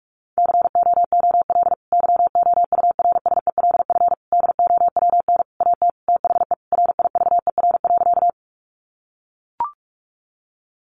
Morse code, heard as a keyed tone, about 44 wpm, 700 Hz. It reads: COOL YOURSELF DOWN AT THE RIVER.
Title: Sentences from Top 1200 Words 45wpm